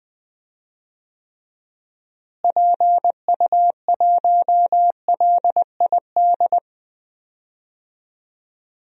Morse code, heard as a keyed tone, about 21 wpm, 700 Hz. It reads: PU1LID
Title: DXCC-100 Call Signs - 6 Characters - Repeated 3x - 20wpm